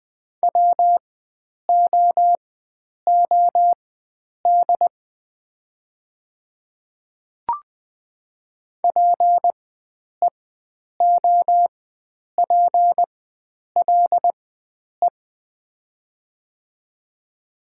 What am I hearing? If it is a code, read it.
Morse code, heard as a keyed tone, about 20 wpm, spaced as for 9 wpm, 700 Hz. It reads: WOOD PEOPLE
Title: 4x Character Spacing - Top 300 Words - 20wpm